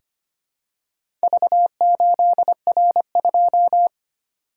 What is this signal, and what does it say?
Morse code, heard as a keyed tone, about 25 wpm, 700 Hz. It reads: V8R2
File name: practice / 25wpm / Four Letter-Number Combinations - 25wpm